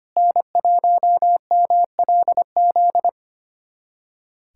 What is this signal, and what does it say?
Morse code, heard as a keyed tone, about 25 wpm, 700 Hz. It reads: N1MLZ
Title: USA Callsign DE Callsign - 5 Characters - 25wpm